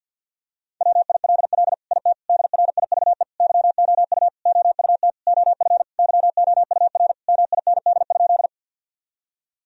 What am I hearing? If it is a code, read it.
Morse code, heard as a keyed tone, about 51 wpm, 700 Hz. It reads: WILL IT DRIVE YOU OUT OF YOUR MIND?
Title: Sentences from Top 1000 Words - Review Part 2 - 50wpm